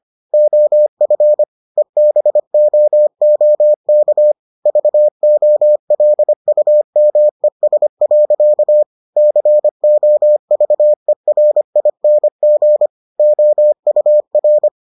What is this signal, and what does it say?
Morse code, heard as a keyed tone, about 25 wpm, 600 Hz. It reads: OF EBOOK VOLUMES. COVERING OUR